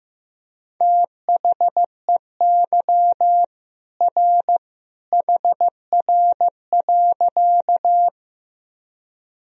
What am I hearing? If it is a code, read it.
Morse code, heard as a keyed tone, about 15 wpm, 700 Hz. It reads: THEY R HR.
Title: Sentences from Top 200 Words and Abbreviations 15wpm